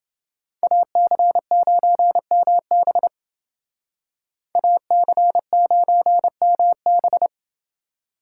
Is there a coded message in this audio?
Yes: Morse code from 0.6 to 7.3 s, heard as a keyed tone, about 29 wpm, 700 Hz. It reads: AC9MB AC9MB